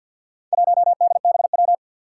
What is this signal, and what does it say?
Morse code, heard as a keyed tone, about 46 wpm, 700 Hz. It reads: 1DBW
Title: Four Challenging Letter-Number Combinations 50wpm